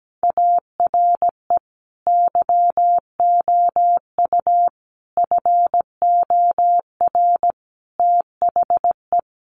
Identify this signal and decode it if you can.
Morse code, heard as a keyed tone, about 17 wpm, 700 Hz. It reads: ARE YOU FOR THE
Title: Sentences from Top 1100 Words 17wpm